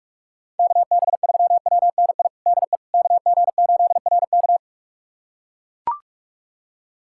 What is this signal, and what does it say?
Morse code, heard as a keyed tone, about 42 wpm, 700 Hz. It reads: KB3WNI DE KG8RK
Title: USA Callsign DE Callsign 45wpm